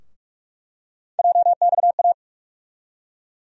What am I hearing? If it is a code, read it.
Morse code, heard as a keyed tone, about 46 wpm, 700 Hz. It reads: JXA
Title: USA Call Sign Postfixes - 3 Letters 45wpm